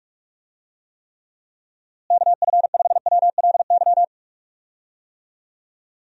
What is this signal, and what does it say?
Morse code, heard as a keyed tone, about 46 wpm, 700 Hz. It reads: KF5WLY
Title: USA Call Signs - 6 Characters - Repeated 3x - 45wpm